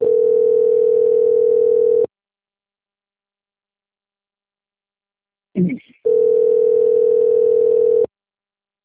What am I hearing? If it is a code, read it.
Ringback tone (North America), 440 + 480 Hz.
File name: out- - - - - .5664